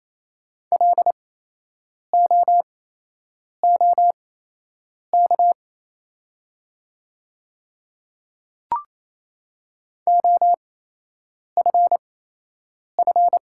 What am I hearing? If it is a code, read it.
Morse code, heard as a keyed tone, about 28 wpm, spaced as for 8 wpm, 700 Hz. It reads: LOOK OFF
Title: 8x Character Spacing - Top 200 Words - 28wpm